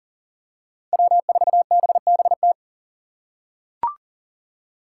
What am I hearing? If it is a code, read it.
Morse code, heard as a keyed tone, about 40 wpm, 700 Hz. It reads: W4BBT
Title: K1USN SST QSO Practice - 40wpm